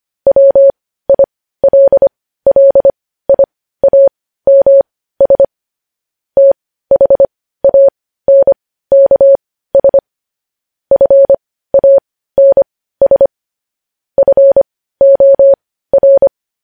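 Morse code, heard as a keyed tone, about 25 wpm, spaced as for 15 wpm, 550 Hz. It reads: WILLIAMS THANKS FANS FOR
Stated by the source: Morse code news headlines practice at 15.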